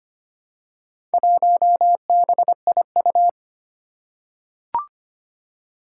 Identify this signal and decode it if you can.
Morse code, heard as a keyed tone, about 25 wpm, 700 Hz. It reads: 1BIU
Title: Four Challenging Letter-Number Combinations 25wpm